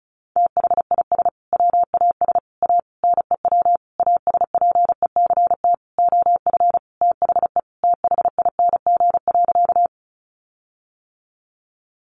Morse code, heard as a keyed tone, about 34 wpm, 700 Hz. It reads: THIS WAS A NEW ASPECT OF THE THING.
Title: Sentences from Top 1300 Words 35wpm